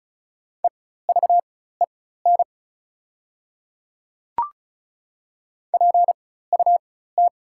Morse code, heard as a keyed tone, about 36 wpm, spaced as for 16 wpm, 690 Hz. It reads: EVEN PUT
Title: 4x Character Spacing - Top 200 Words - 35wpm